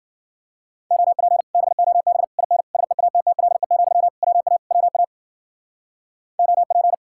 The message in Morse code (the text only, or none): OP BOB IN SEATTLE, WA WA OP